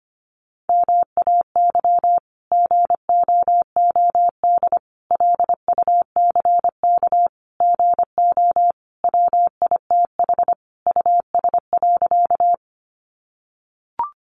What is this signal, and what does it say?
Morse code, heard as a keyed tone, about 25 wpm, 700 Hz. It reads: MAY GOOD LUCK GO WITH US.
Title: Sentences from Top 1500 Words 25wpm